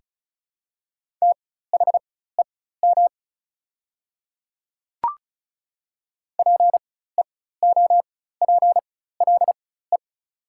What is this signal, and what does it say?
Morse code, heard as a keyed tone, about 34 wpm, spaced as for 16 wpm, 700 Hz. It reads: THEM PEOPLE
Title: 4x Character Spacing - Top 200 Words - 35wpm